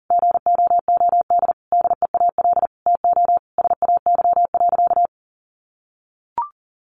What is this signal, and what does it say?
Morse code, heard as a keyed tone, about 40 wpm, 700 Hz. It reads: GOOD DEAL TO SAY.
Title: Sentences from Top 1000 Words 40wpm